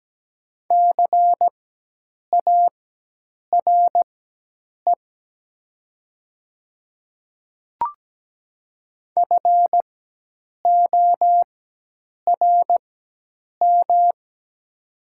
Morse code, heard as a keyed tone, about 17 wpm, spaced as for 8 wpm, 700 Hz. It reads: CARE FORM
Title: 4x Character Spacing - Top 500 Words - 17wpm